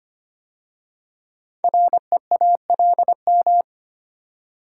Morse code, heard as a keyed tone, about 25 wpm, 700 Hz. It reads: REALM
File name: 5 Letter Words - First 3 Letters Given - 25wpm